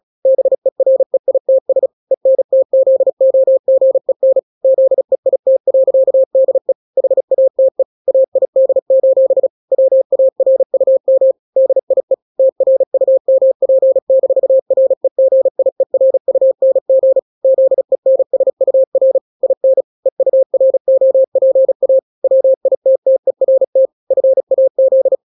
Morse code, heard as a keyed tone, about 36 wpm, 530 Hz. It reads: BEREITS ENTZOGEN ZEIT.DE HATE AID: WARUM DIE TRUMP-REGIERUNG ZENSUR IN EUROPA WITTERT FAZ
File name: daily / 35wpm